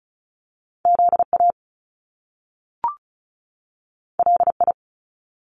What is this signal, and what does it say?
Morse code, heard as a keyed tone, about 36 wpm, 700 Hz. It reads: ZA LI
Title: Top Two Letter Combinations in Words 35wpm